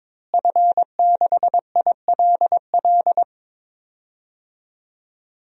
Morse code, heard as a keyed tone, about 21 wpm, 700 Hz. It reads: F6ILL
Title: DXCC-100 Call Signs - 5 Characters - Repeated 2x - 22wpm